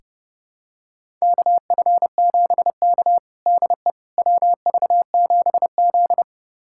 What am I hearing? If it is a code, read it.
Morse code, heard as a keyed tone, about 32 wpm, 700 Hz. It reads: KF7K DE WV7Z
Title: USA Callsign DE Callsign - 4 Characters - 30wpm